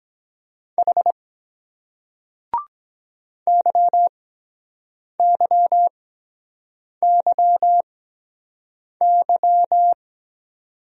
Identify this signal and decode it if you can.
Morse code, heard as a keyed tone, about 23 wpm, spaced as for 7 wpm, 700 Hz. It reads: H YYYY